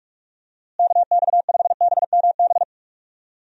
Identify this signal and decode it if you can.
Morse code, heard as a keyed tone, about 42 wpm, 700 Hz. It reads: KX5BMB
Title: USA Call Signs - 6 Characters - Repeated 2x - 45wpm